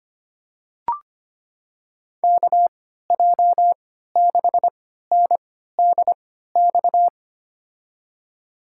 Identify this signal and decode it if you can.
Morse code, heard as a keyed tone, about 25 wpm, spaced as for 14 wpm, 700 Hz. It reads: KJ6NDX
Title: USA Call Signs - 6 Characters - 3x Character Spacing - 25wpm